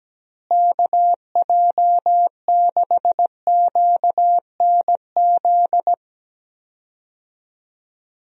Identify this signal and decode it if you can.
Morse code, heard as a keyed tone, about 17 wpm, 700 Hz. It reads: KJ6QNZ
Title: USA Call Signs - 6 Characters - Repeated 2x - 17wpm